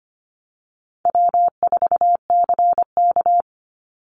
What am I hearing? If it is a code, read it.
Morse code, heard as a keyed tone, about 25 wpm, 700 Hz. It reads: W4CK
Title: Easy Call Signs - 25wpm